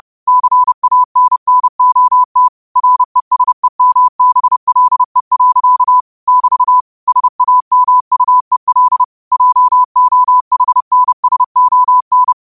Morse code, heard as a keyed tone, about 30 wpm, 1000 Hz. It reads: CANNOT RESEMBLE. <BT> SAMUEL JOHNSON